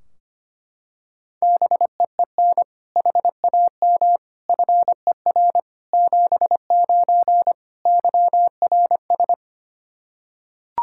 Morse code, heard as a keyed tone, about 25 wpm, 700 Hz. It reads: BEEN HAM FER 79 YRS
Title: Practice QSO - Exchange 3 - More Info - 25wpm